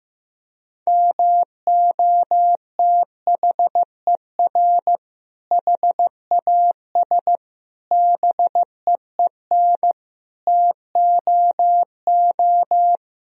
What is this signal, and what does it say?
Morse code, heard as a keyed tone, about 16 wpm, 700 Hz. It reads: MOTHER HAS BEEN TOO
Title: Sentences from Top 1000 Words - Review Part 1 - 15wpm